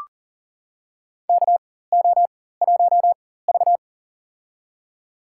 Morse code, heard as a keyed tone, about 40 wpm, spaced as for 19 wpm, 700 Hz. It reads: KO1V
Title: USA Call Signs - 4x Character Spacing - 40wpm